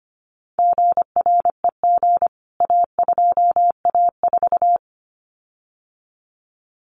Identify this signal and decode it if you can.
Morse code, heard as a keyed tone, about 25 wpm, 700 Hz. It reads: GREG A2A4
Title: CWT Exchange 25wpm